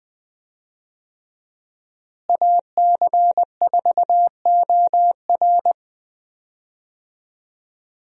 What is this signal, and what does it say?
Morse code, heard as a keyed tone, about 21 wpm, 700 Hz. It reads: AC4OR
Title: USA Call Signs - 5 Characters - Repeated 3x - 20wpm